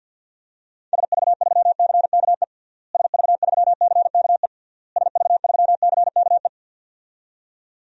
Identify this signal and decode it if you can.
Morse code, heard as a keyed tone, about 53 wpm, 700 Hz. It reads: SV3CKE SV3CKE SV3CKE
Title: DXCC-100 Call Signs - 6 Characters - Repeated 3x - 50wpm